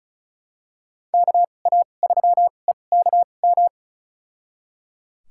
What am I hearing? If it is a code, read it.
Morse code, heard as a keyed tone, about 36 wpm, spaced as for 25 wpm, 700 Hz. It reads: KA3EKM
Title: USA Call Signs - 6 Characters - 2x Character Spacing - 35wpm